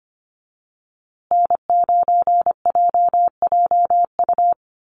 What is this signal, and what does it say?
Morse code, heard as a keyed tone, about 25 wpm, 700 Hz. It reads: N9JJU
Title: USA Call Signs Repeated 3x 25wpm